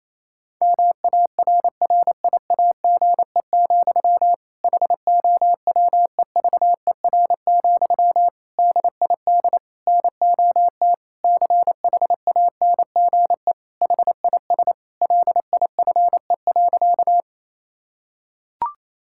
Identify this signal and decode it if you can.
Morse code, heard as a keyed tone, about 28 wpm, 700 Hz. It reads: MARRIAGE, HOWEVER, DID NOT CHANGE HIS LIFE.
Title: Sentences from Top 1000 Words - Review Part 5 - 28wpm